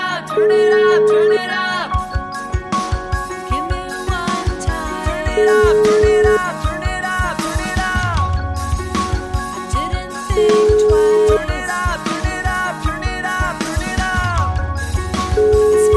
Ringback tone (Europe, 425 Hz), with noise.